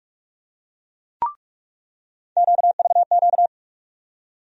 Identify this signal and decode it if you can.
Morse code, heard as a keyed tone, about 46 wpm, 700 Hz. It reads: QVQ
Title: USA Call Sign Postfixes - 3 Letters 45wpm